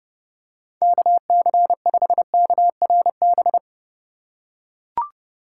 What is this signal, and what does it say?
Morse code, heard as a keyed tone, about 30 wpm, 700 Hz. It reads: KC5KRB
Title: USA Call Signs - 4x Character Spacing - 30wpm